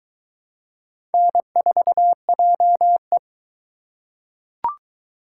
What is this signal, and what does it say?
Morse code, heard as a keyed tone, about 20 wpm, 700 Hz. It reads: N4JE